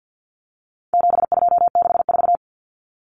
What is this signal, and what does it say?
Morse code, heard as a keyed tone, about 48 wpm, 700 Hz. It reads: 7264
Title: Numbers - 4-Digit 50wpm